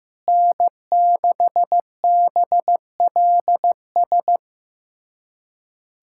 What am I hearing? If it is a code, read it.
Morse code, heard as a keyed tone, about 15 wpm, 700 Hz. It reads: N6BLS
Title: USA Callsign DE Callsign - 5 Characters - 15wpm